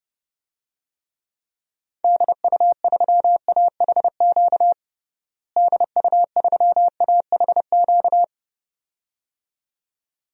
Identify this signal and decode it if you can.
Morse code, heard as a keyed tone, about 30 wpm, 700 Hz. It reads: DU3AHQ DU3AHQ
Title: DXCC-100 Call Signs - 6 Characters - Repeated 2x - 30wpm